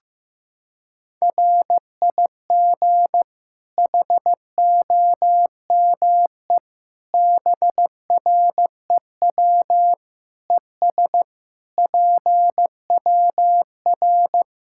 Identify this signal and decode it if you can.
Morse code, heard as a keyed tone, about 16 wpm, 700 Hz. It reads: RIG HOME BREW ES PWR
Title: Practice QSO - Exchange 2 - Info - 15wpm